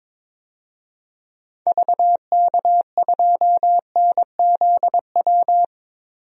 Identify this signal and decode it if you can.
Morse code, heard as a keyed tone, about 22 wpm, 700 Hz. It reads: VK2NZW
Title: DXCC-100 Call Signs - 6 Characters - 22wpm